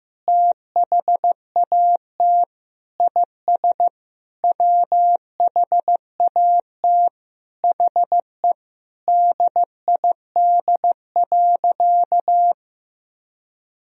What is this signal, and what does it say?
Morse code, heard as a keyed tone, about 15 wpm, 700 Hz. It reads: THAT IS WHAT HE DID.
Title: Sentences from Top 1000 Words - Review Part 4 - 15wpm